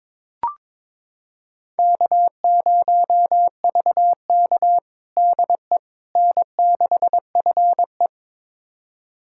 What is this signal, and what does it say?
Morse code, heard as a keyed tone, about 22 wpm, 700 Hz. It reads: K0VK DE N6FE
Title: USA Callsign DE Callsign - 4 Characters - 22wpm